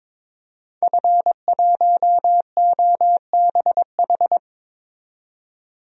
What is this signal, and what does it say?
Morse code, heard as a keyed tone, about 22 wpm, 700 Hz. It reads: F1OBH
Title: DXCC-100 Call Signs - 5 Characters - 22wpm